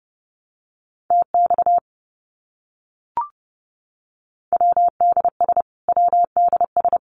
Morse code, heard as a keyed tone, about 30 wpm, 700 Hz. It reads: TX WDS WDS
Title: Abbreviations Repeated 3x - 30wpm